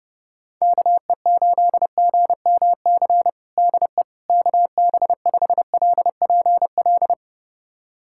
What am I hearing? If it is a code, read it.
Morse code, heard as a keyed tone, about 30 wpm, 700 Hz. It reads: KE8GMC DE KB5LPL
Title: USA Callsign DE Callsign - 6 Characters - 30wpm